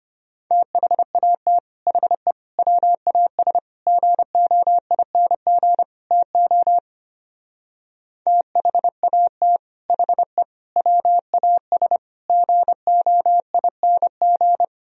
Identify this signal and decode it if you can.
Morse code, heard as a keyed tone, about 32 wpm, 700 Hz. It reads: THAT HE WAS GOING TO THAT HE WAS GOING